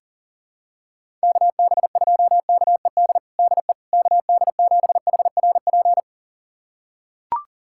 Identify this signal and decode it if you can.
Morse code, heard as a keyed tone, about 40 wpm, 700 Hz. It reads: KB2KED DE KD7HRP
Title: USA Callsign DE Callsign 40wpm